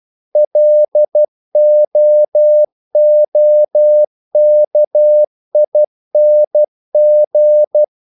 Morse code, heard as a keyed tone, about 12 wpm, 600 Hz. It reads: LOOKING
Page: English quotations in CW (telegraphy - morse)